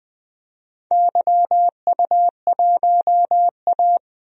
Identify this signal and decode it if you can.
Morse code, heard as a keyed tone, about 21 wpm, 700 Hz. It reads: YU1A
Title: DXCC-100 Call Signs - 4 Characters - 20wpm